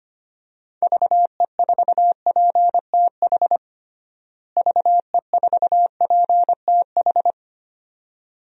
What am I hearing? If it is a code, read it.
Morse code, heard as a keyed tone, about 25 wpm, 700 Hz. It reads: VE4PTH VE4PTH
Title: DXCC-100 Call Signs - 6 Characters - Repeated 2x - 25wpm